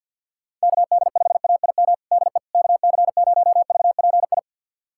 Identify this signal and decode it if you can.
Morse code, heard as a keyed tone, about 50 wpm, 700 Hz. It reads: KD5AIM DE KC0VPI
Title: USA Callsign DE Callsign 50wpm